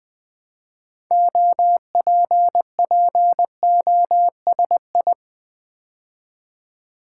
Morse code, heard as a keyed tone, about 20 wpm, 700 Hz. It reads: OPPOSI
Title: Build A Word - 2 Letters At A Time - 8-Letter Words - 20wpm